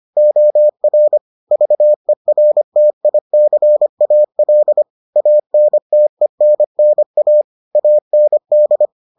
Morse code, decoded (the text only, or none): OR VERTICAL ANTENNA AND